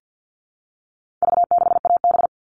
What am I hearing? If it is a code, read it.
Morse code, heard as a keyed tone, about 50 wpm, 700 Hz. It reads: V6AB
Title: All Cut Numbers - 4-Digit 50wpm